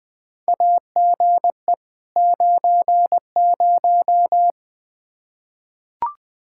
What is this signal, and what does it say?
Morse code, heard as a keyed tone, about 21 wpm, 700 Hz. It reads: AGE 90
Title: K1USN SST QSO Practice 20wpm